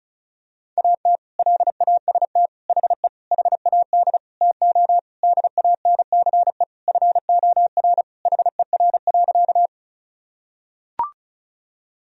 Morse code, heard as a keyed tone, about 36 wpm, 700 Hz. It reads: AT LAST HE HAD TO DANCE FOR HER.
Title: Sentences from Top 1000 Words - Review Part 4 - 35wpm